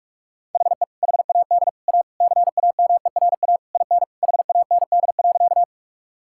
Morse code, heard as a keyed tone, about 42 wpm, 700 Hz. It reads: HE HAD A CAMERA IN HAND.
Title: Sentences from Top 1000 Words - Review Part 5 - 45wpm